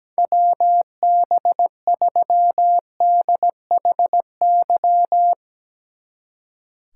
Morse code, heard as a keyed tone, about 18 wpm, 700 Hz. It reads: WB3DHY
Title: Speed Racing - USA Call Signs 17wpm